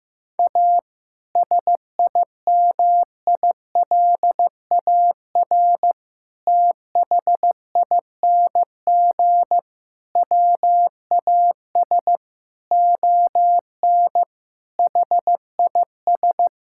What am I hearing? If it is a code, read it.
Morse code, heard as a keyed tone, about 16 wpm, 700 Hz. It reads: A SIMILAR THING WAS ON HIS
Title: Sentences from Top 1000 Words - Review Part 5 - 15wpm